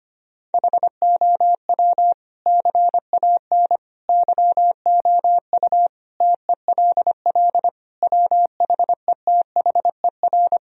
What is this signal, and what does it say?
Morse code, heard as a keyed tone, about 25 wpm, 700 Hz. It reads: HOW CAN YOU TELL WHETHER
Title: Sentences from Top 1000 Words - Review Part 3 - 25wpm